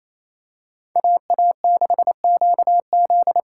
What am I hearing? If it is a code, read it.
Morse code, heard as a keyed tone, about 27 wpm, 700 Hz. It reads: AA6QZ
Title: USA Callsign DE Callsign - 5 Characters - 28wpm